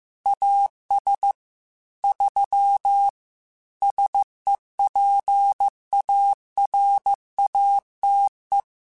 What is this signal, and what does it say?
Morse code, heard as a keyed tone, about 15 wpm, 800 Hz. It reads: AS 3 SEPARATE
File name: Technical Subjects / Technical Subjects-mp3s / Sw Pwr Sup @13